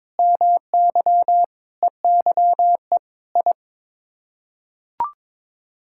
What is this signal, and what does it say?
Morse code, heard as a keyed tone, about 22 wpm, 700 Hz. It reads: MY EYE I
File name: practice / 22wpm / Sets of 8 Words - 22wpm